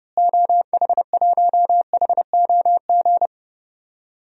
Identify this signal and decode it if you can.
Morse code, heard as a keyed tone, about 30 wpm, 700 Hz. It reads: OH1HOG
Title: DXCC-100 Call Signs - 6 Characters - Repeated 2x - 30wpm